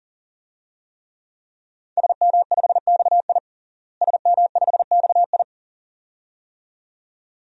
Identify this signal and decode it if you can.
Morse code, heard as a keyed tone, about 40 wpm, 700 Hz. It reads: SM5XI SM5XI
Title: DXCC-100 Call Signs - 5 Characters - Repeated 2x - 40wpm